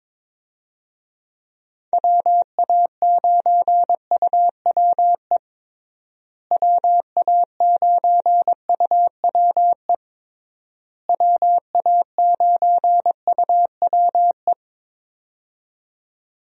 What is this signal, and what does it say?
Morse code, heard as a keyed tone, about 22 wpm, 700 Hz. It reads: WA9UWE WA9UWE WA9UWE
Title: USA Call Signs - 6 Characters - Repeated 3x - 22wpm